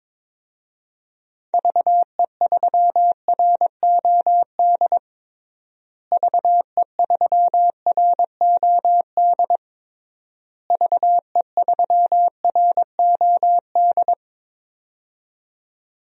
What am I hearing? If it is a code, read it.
Morse code, heard as a keyed tone, about 22 wpm, 700 Hz. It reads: VE3ROD VE3ROD VE3ROD
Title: DXCC-100 Call Signs - 6 Characters - Repeated 3x - 22wpm